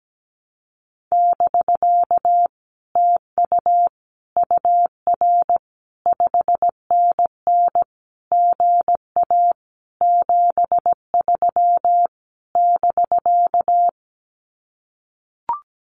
Morse code, heard as a keyed tone, about 17 wpm, 700 Hz. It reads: <BK> TU UR 5NN GA 73 <BK>
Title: Common QSO Phrases Encore 17wpm